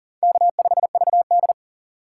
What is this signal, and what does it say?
Morse code, heard as a keyed tone, about 40 wpm, 700 Hz. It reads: K5VD
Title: USA Call Signs - 4 Characters - 2x Character Spacing - 40wpm